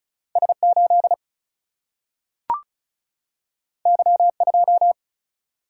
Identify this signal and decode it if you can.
Morse code, heard as a keyed tone, about 36 wpm, 700 Hz. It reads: S8 Y2
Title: Single-Letter Single-Number 35wpm